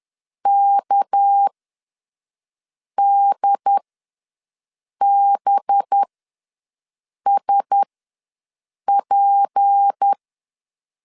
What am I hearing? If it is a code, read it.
Morse code, heard as a keyed tone, about 10 wpm, spaced as for 5 wpm, 800 Hz. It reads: KDBSP
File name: LEZIONI / 6 PKX0 / 6 PK ws 10wpm 3745